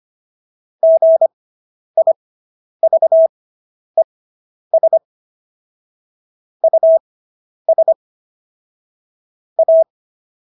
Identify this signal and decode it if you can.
Morse code, heard as a keyed tone, about 25 wpm, spaced as for 10 wpm, 650 Hz. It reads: GIVES US A